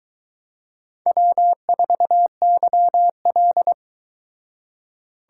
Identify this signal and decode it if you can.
Morse code, heard as a keyed tone, about 22 wpm, 700 Hz. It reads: W4YL